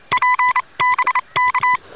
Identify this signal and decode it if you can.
Morse code, heard as a keyed tone, about 27 wpm, spaced as for 23 wpm, 1000 Hz. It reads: PBK
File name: Expedition / 12.01.2013 RK3WWZ UA1CEG p test